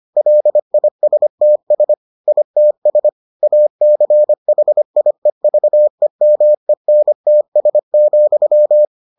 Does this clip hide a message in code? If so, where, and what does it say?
Yes: Morse code from 0.2 to 8.8 s, heard as a keyed tone, about 25 wpm, 600 Hz. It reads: LISTS ITS ACHIEVEMENTS,